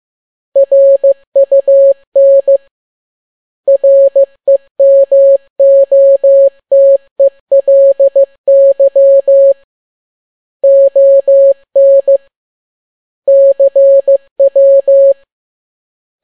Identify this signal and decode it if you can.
Morse code, heard as a keyed tone, about 15 wpm, 550 Hz. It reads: RUN REMOTELY ON CW